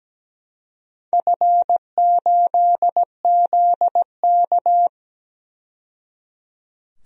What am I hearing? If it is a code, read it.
Morse code, heard as a keyed tone, about 17 wpm, 700 Hz. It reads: F8ZK
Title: DXCC-100 Call Signs - 4 Characters - 17wpm